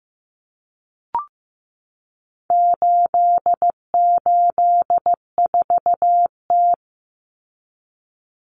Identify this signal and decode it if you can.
Morse code, heard as a keyed tone, about 15 wpm, 700 Hz. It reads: 884T
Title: Common Cut Numbers - 4-Digit 15wpm